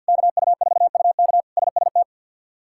Morse code, heard as a keyed tone, about 53 wpm, 700 Hz. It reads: KU4UK SST
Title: K1USN SST QSO Practice - 50wpm